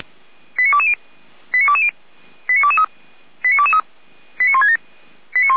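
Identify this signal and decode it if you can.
Five-tone selective call (ZVEI), heard as quick runs of five tones, tones of about 70 ms. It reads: 89220 89220 89223 89223 89167
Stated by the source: ZVEI five-tone selective call received with an SDR, feuerwehr-orlamuende.org fire brigade.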